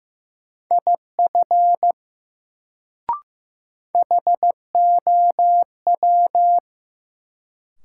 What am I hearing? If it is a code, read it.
Morse code, heard as a keyed tone, about 15 wpm, 700 Hz. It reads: IF HOW
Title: Sentences from Top 100 Words - New Word Drill - 15wpm